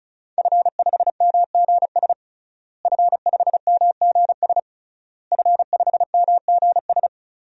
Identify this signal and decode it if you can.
Morse code, heard as a keyed tone, about 36 wpm, 700 Hz. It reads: F5MGS F5MGS F5MGS
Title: DXCC-100 Call Signs - 5 Characters - Repeated 3x - 35wpm